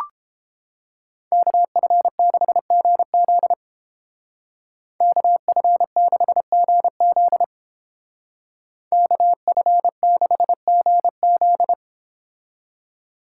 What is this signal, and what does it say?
Morse code, heard as a keyed tone, about 30 wpm, 700 Hz. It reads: KF6GZ KF6GZ KF6GZ